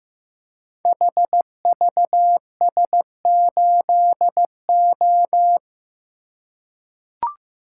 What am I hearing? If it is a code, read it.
Morse code, heard as a keyed tone, about 15 wpm, 700 Hz. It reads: HVS8O
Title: Code Groups - 5 Letter:Numbers - 15wpm